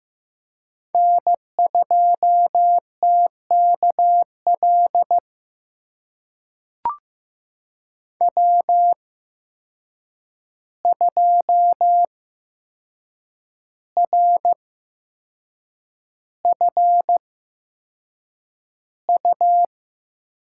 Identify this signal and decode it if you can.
Morse code, heard as a keyed tone, about 15 wpm, spaced as for 6 wpm, 700 Hz. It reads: N2TKL W2RFU
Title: USA Call Signs - 5 Characters - 8x Character Spacing - 15wpm